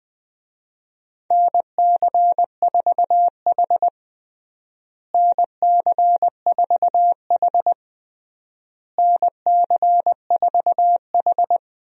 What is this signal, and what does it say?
Morse code, heard as a keyed tone, about 21 wpm, 700 Hz. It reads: NC4H NC4H NC4H
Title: USA Call Signs - 4 Characters - Repeated 3x - 20wpm